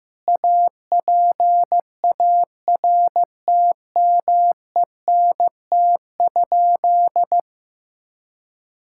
Morse code, heard as a keyed tone, about 15 wpm, 700 Hz. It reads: APARTMENT?
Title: Sentences from Top 1000 Words - Review Part 6 - 15wpm